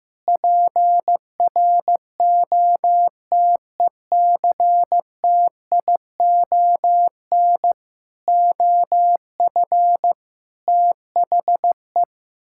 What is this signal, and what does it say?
Morse code, heard as a keyed tone, about 15 wpm, 700 Hz. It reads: PROTECTION OF THE
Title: Sentences from Top 1600 Words 15wpm